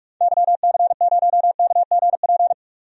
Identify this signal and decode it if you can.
Morse code, heard as a keyed tone, about 47 wpm, 690 Hz. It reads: YC0KGP
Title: DXCC-100 Call Signs - 6 Characters - 45wpm